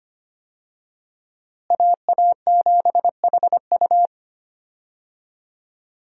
Morse code, heard as a keyed tone, about 25 wpm, 700 Hz. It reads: AA7HU
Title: USA Call Signs - 5 Characters - Repeated 2x - 25wpm